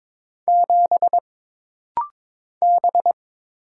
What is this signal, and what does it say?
Morse code, heard as a keyed tone, about 22 wpm, 700 Hz. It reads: 7 B
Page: This rapid-fire timing is geared to push your ICR (Instant Character Recognition) skills.